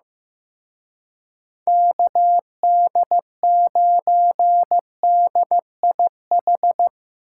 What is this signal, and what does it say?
Morse code, heard as a keyed tone, about 17 wpm, 700 Hz. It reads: KD9DIH
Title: Speed Racing - USA 2x3 Call Signs 15wpm